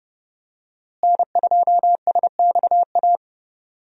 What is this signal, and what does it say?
Morse code, heard as a keyed tone, about 30 wpm, 700 Hz. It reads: N2SXA
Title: USA Call Signs - 5 Characters - Repeated 2x - 30wpm